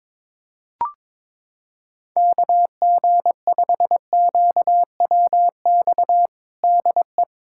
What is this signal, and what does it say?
Morse code, heard as a keyed tone, about 22 wpm, 700 Hz. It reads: KG5QWX DE
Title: USA Callsign DE Callsign - 6 Characters - 22wpm